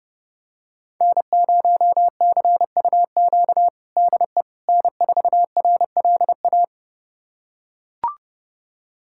Morse code, heard as a keyed tone, about 30 wpm, 700 Hz. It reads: N0CUQ DE N4RLA
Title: USA Callsign DE Callsign - 5 Characters - 30wpm